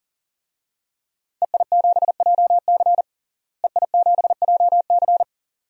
Morse code, heard as a keyed tone, about 40 wpm, 700 Hz. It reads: EI7JC EI7JC
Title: DXCC-100 Call Signs - 5 Characters - Repeated 2x - 40wpm